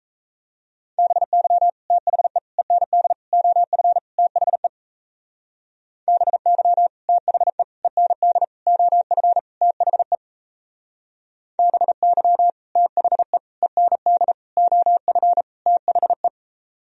Morse code, heard as a keyed tone, about 37 wpm, 700 Hz. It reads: BY THE END OF THE BY THE END OF THE BY THE END OF THE